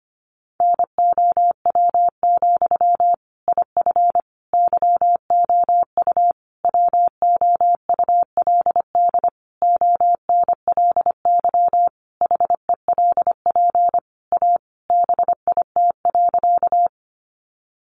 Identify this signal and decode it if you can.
Morse code, heard as a keyed tone, about 25 wpm, 700 Hz. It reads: NOW, IF YOU WOULD ONLY HELP A BIT.
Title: Sentences from Top 900 Words 25wpm